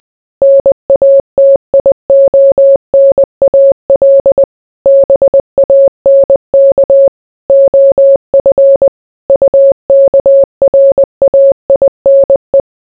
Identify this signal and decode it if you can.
Morse code, heard as a keyed tone, about 20 wpm, 550 Hz. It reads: NATIONAL BANK OF UKRAINE